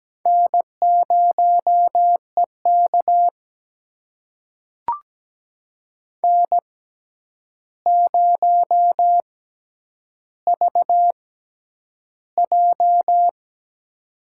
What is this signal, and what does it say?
Morse code, heard as a keyed tone, about 17 wpm, spaced as for 9 wpm, 700 Hz. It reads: N0EK N0VJ
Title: USA Call Signs - 4 Characters - 6x Character Spacing - 17wpm